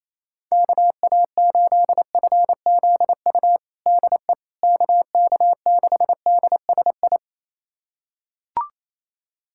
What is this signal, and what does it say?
Morse code, heard as a keyed tone, about 28 wpm, 700 Hz. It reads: KA8FZU DE KK6DSI
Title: USA Callsign DE Callsign - 6 Characters - 28wpm